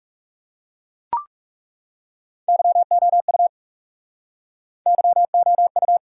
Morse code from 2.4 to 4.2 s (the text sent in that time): YOU